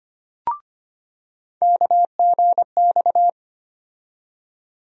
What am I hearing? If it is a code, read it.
Morse code, heard as a keyed tone, about 25 wpm, 700 Hz. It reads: KGX
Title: USA Call Sign Postfixes - 3 Letters 25wpm